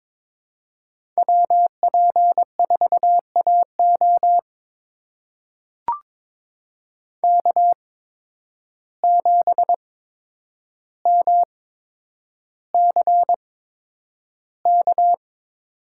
Morse code, heard as a keyed tone, about 22 wpm, spaced as for 9 wpm, 700 Hz. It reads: WP4AO K7MCK
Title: USA Call Signs - 5 Characters - 8x Character Spacing - 22wpm